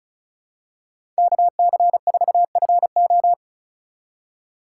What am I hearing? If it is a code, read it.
Morse code, heard as a keyed tone, about 34 wpm, 700 Hz. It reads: KC4FO
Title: USA Call Signs - 5 Characters - 4x Character Spacing - 35wpm